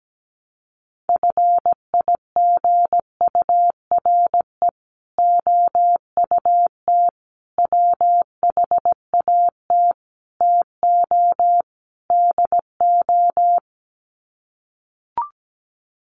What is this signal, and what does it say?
Morse code, heard as a keyed tone, about 17 wpm, 700 Hz. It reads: FIGURE OUT WHAT TO DO
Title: Sets of 5 Words Encore2 17wpm